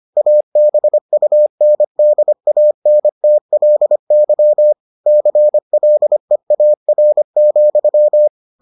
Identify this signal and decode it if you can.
Morse code, heard as a keyed tone, about 25 wpm, 600 Hz. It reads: ABUNDANTLY CLEAR,